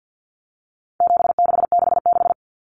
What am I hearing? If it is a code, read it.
Morse code, heard as a keyed tone, about 50 wpm, 700 Hz. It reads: 7666
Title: Numbers - 4-Digit 50wpm